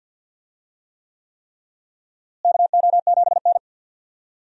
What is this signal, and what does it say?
Morse code, heard as a keyed tone, about 47 wpm, 700 Hz. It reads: KO7N
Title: USA Call Signs - 4 Characters - Repeated 3x - 50wpm